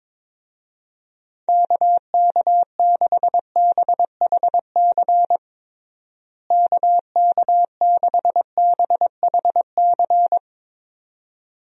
Morse code, heard as a keyed tone, about 22 wpm, 700 Hz. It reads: KK6BHC KK6BHC
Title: USA Call Signs - 6 Characters - Repeated 2x - 22wpm